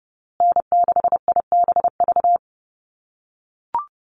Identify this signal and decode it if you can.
Morse code, heard as a keyed tone, about 30 wpm, 700 Hz. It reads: N6IBV
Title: USA Call Signs Repeated 2x 30wpm